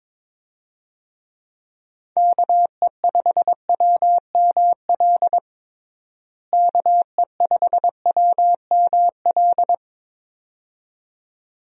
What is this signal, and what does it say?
Morse code, heard as a keyed tone, about 22 wpm, 700 Hz. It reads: KE5WML KE5WML
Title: USA Call Signs - 6 Characters - Repeated 2x - 22wpm